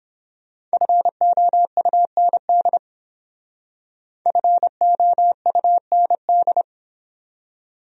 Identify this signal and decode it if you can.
Morse code, heard as a keyed tone, about 29 wpm, 700 Hz. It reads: FOUND FOUND